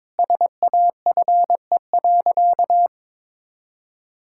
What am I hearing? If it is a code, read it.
Morse code, heard as a keyed tone, about 22 wpm, 700 Hz. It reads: SAFE.
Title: Sentences from Top 1600 Words 22wpm